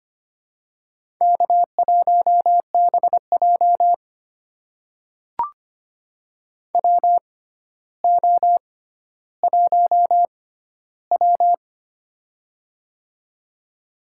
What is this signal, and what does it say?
Morse code, heard as a keyed tone, about 25 wpm, spaced as for 13 wpm, 700 Hz. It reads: K1BJ WO1W
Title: USA Call Signs - 4 Characters - 6x Character Spacing - 25wpm